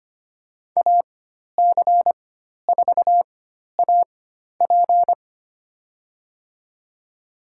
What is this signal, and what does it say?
Morse code, heard as a keyed tone, about 25 wpm, spaced as for 12 wpm, 700 Hz. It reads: AC4AP
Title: USA Call Signs - 5 Characters - 4x Character Spacing - 25wpm